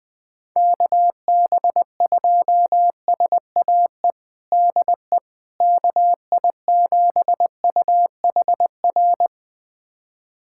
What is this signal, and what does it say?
Morse code, heard as a keyed tone, about 20 wpm, 700 Hz. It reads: KB2SAE DE KI7UHR
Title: USA Callsign DE Callsign - 6 Characters - 20wpm